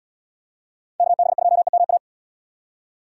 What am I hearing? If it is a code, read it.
Morse code, heard as a keyed tone, about 74 wpm, 690 Hz. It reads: KB2EMA